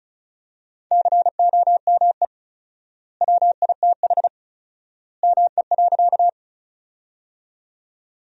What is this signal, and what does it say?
Morse code, heard as a keyed tone, about 36 wpm, 700 Hz. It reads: COME WITH ME.
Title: 4x Word-Spacing - Sentences from Top 200 Words - 35wpm